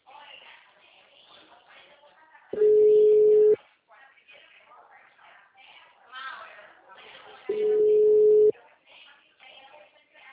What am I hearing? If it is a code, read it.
Ringback tone (Europe, 425 Hz), with speech.
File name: out- -2015- - - .36